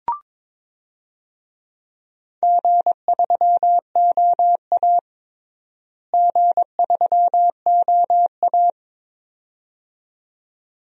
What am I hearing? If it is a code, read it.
Morse code, heard as a keyed tone, about 22 wpm, 700 Hz. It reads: G3OA G3OA
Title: DXCC-100 Call Signs - 4 Characters - Repeated 2x - 22wpm